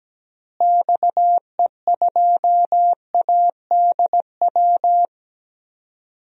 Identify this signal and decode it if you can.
Morse code, heard as a keyed tone, about 17 wpm, 700 Hz. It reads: XE2ADW
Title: DXCC-100 Call Signs - 6 Characters - Repeated 3x - 17wpm